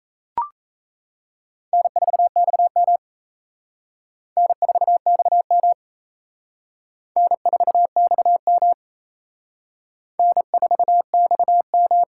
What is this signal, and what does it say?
Morse code, heard as a keyed tone, about 34 wpm, 700 Hz. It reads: N4XM N4XM N4XM N4XM
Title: Speed Racing - USA 1x2 Call Signs 28wpm